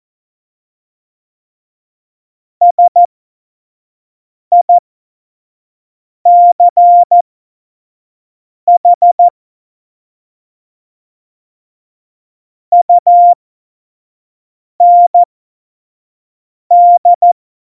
Morse code, heard as a keyed tone, about 14 wpm, spaced as for 5 wpm, 700 Hz. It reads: SICH UND
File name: Morsekurs / Witze / Witz 23. April 2019, 25 F70